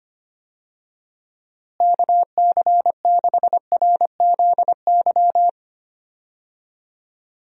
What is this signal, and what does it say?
Morse code, heard as a keyed tone, about 25 wpm, 700 Hz. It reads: KC6RZY
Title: USA Call Signs - 6 Characters - Repeated 3x - 25wpm